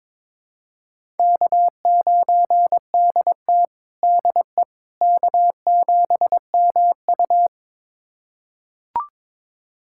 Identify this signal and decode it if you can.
Morse code, heard as a keyed tone, about 22 wpm, 700 Hz. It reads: K9DT DE K7MU
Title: USA Callsign DE Callsign - 4 Characters - 22wpm